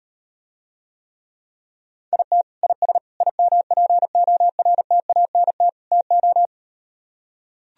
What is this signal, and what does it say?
Morse code, heard as a keyed tone, about 38 wpm, 700 Hz. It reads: IT IS IMPORTANT TO